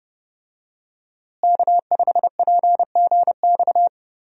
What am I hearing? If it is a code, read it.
Morse code, heard as a keyed tone, about 30 wpm, 700 Hz. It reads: K5PGX
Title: USA Call Signs - 5 Characters - Repeated 2x - 30wpm